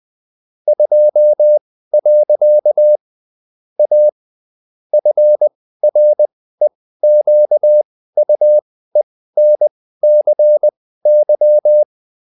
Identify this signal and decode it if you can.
Morse code, heard as a keyed tone, about 20 wpm, spaced as for 14 wpm, 600 Hz. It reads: 2. A FREQUENCY